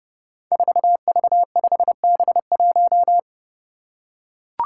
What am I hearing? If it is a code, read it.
Morse code, heard as a keyed tone, about 30 wpm, 700 Hz. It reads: 4V5B1
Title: Five Challenging Letter-Number Combinations 30wpm